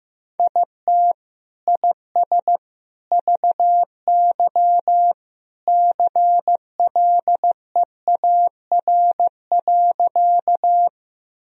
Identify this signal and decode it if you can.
Morse code, heard as a keyed tone, about 15 wpm, 700 Hz. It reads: IT IS VY CLEAR.
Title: Sentences from Top 500 Words and Abbreviations 15wpm